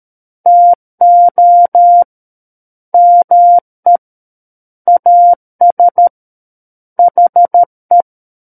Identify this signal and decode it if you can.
Morse code, heard as a keyed tone, about 13 wpm, 700 Hz. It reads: TO ME AS HE
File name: Book One / 04-Chapter 4 - 13 WPM